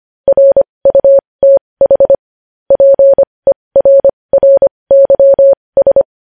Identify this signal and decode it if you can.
Morse code, heard as a keyed tone, about 25 wpm, spaced as for 20 wpm, 550 Hz. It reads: RUTH PERRYS
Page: News Headlines in Morse Code at 20 WPM